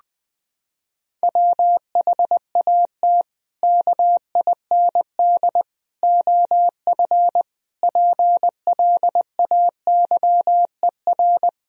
Morse code, heard as a keyed tone, about 21 wpm, 700 Hz. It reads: WHAT KIND OF PLAYER
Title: Sentences from Top 1000 Words - Review Part 4 - 20wpm